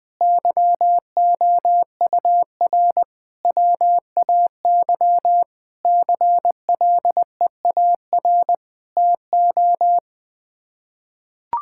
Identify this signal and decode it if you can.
Morse code, heard as a keyed tone, about 20 wpm, 700 Hz. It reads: YOUR WAY CLEAR TO
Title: Sets of 9 Words - 20wpm